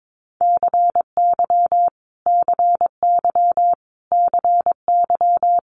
Morse code, heard as a keyed tone, about 22 wpm, 700 Hz. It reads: CY CY CY
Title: Multi-letter Phonemes Repeated 3x 22wpm